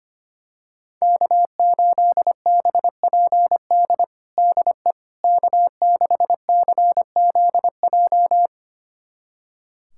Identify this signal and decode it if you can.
Morse code, heard as a keyed tone, about 25 wpm, 700 Hz. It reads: K8BPD DE K6CZJ
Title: USA Callsign DE Callsign - 5 Characters - 25wpm